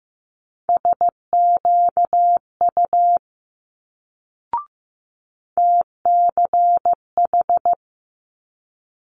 Morse code, heard as a keyed tone, about 15 wpm, 700 Hz. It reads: SQU TCH
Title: Multi-letter Phonemes Quick Fire 15wpm